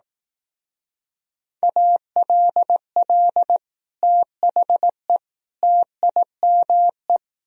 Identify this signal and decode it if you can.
Morse code, heard as a keyed tone, about 18 wpm, 700 Hz. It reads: ALL THE TIME
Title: Speed-Racing - Sets of 3 Words 15wpm